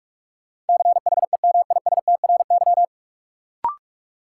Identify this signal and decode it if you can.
Morse code, heard as a keyed tone, about 46 wpm, 700 Hz. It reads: CHEMISTRY
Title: Sentences from Top 1000 Words - Review Part 6 - New Word Drill - 45wpm